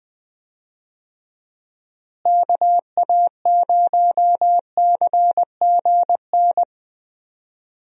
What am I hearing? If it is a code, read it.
Morse code, heard as a keyed tone, about 20 wpm, 700 Hz. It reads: KA0CGN
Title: USA Call Signs - 6 Characters - Repeated 3x - 20wpm